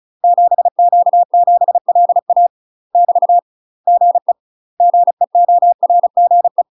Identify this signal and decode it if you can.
Morse code, heard as a keyed tone, about 34 wpm, 700 Hz. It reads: 7Q7LA <BT> GE GEORGE
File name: UK MASTER QSOs TRAINING/35WPM